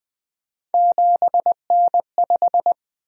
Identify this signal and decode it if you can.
Morse code, heard as a keyed tone, about 20 wpm, 700 Hz. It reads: 7N5
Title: Three Letter-Number Combinations - 20wpm